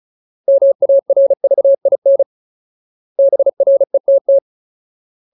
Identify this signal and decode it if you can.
Morse code, heard as a keyed tone, about 34 wpm, 550 Hz. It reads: MARVIN BRETT
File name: Ultimate Word List / 35WPM Ultimate